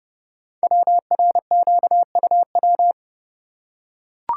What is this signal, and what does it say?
Morse code, heard as a keyed tone, about 30 wpm, 690 Hz. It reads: WRQUW
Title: Code Groups - 5 Letters - 30wpm